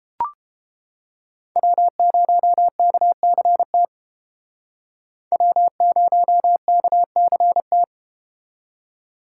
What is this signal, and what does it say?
Morse code, heard as a keyed tone, about 31 wpm, 700 Hz. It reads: W0KCT W0KCT